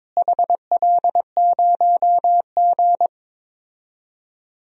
Morse code, heard as a keyed tone, about 22 wpm, 700 Hz. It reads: HL0G
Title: DXCC-100 Call Signs - 4 Characters - Repeated 3x - 22wpm